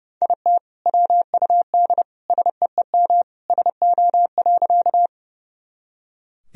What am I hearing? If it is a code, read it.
Morse code, heard as a keyed tone, about 30 wpm, 700 Hz. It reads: IT WUD SEEM SO.
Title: Sentences from Top 300 Words and Abbreviations 30wpm